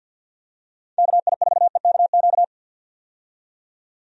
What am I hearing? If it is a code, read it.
Morse code, heard as a keyed tone, about 48 wpm, 700 Hz. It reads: KI4EKQ
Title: USA Call Signs - 6 Characters - 50wpm